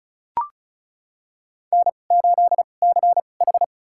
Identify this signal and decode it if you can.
Morse code, heard as a keyed tone, about 36 wpm, spaced as for 25 wpm, 700 Hz. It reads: N8CH
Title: USA Call Signs - 4 Characters - 2x Character Spacing - 35wpm